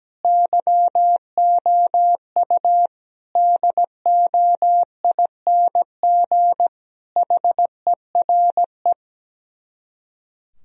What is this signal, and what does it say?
Morse code, heard as a keyed tone, about 17 wpm, 700 Hz. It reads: YOU DOING HERE
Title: Speed-Racing - Sets of 5 Words 17wpm